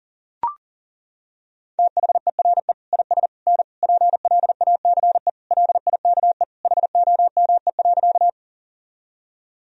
Morse code, heard as a keyed tone, about 40 wpm, 700 Hz. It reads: THERE IS N PLACE LIKE HOME.
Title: Sentences from Top 200 Words and Abbreviations 40wpm